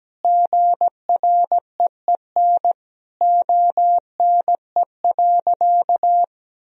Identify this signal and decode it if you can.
Morse code, heard as a keyed tone, about 17 wpm, 700 Hz. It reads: GREEN ONE.
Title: Sentences from Top 1000 Words - Review Part 2 - 17wpm